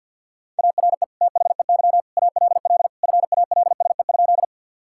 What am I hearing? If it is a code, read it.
Morse code, heard as a keyed tone, about 52 wpm, 700 Hz. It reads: ARE THEY ALL FALSE?
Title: Sentences from Top 1600 Words 50wpm